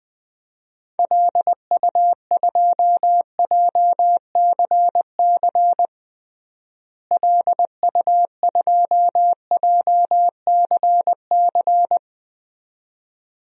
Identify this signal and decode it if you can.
Morse code, heard as a keyed tone, about 20 wpm, 700 Hz. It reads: LU2JCC LU2JCC
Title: DXCC-100 Call Signs - 6 Characters - Repeated 2x - 20wpm